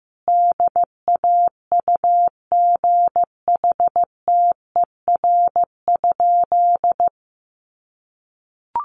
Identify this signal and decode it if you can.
Morse code, heard as a keyed tone, about 15 wpm, 700 Hz. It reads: DAUGHTER?
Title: Sentences from Top 1500 Words 15wpm